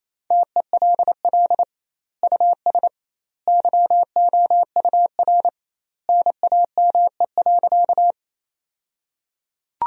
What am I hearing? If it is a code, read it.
Morse code, heard as a keyed tone, about 28 wpm, 700 Hz. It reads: TELL US YOUR NAME.
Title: 2x Word-Spacing - Sentences from Top 200 Words - 28wpm